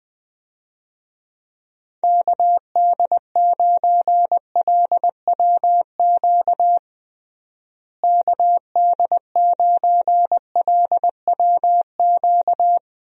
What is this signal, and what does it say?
Morse code, heard as a keyed tone, about 20 wpm, 700 Hz. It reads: KD9LWQ KD9LWQ
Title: USA Call Signs - 6 Characters - Repeated 3x - 20wpm